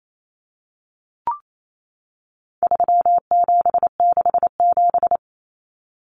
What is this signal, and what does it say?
Morse code, heard as a keyed tone, about 28 wpm, 700 Hz. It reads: 3767
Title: Numbers - 4-Digit 28wpm